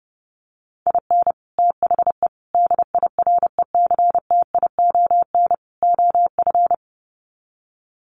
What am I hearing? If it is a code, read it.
Morse code, heard as a keyed tone, about 28 wpm, 700 Hz. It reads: IN THE DIRECTION OF
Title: Sets of 4 Words 30wpm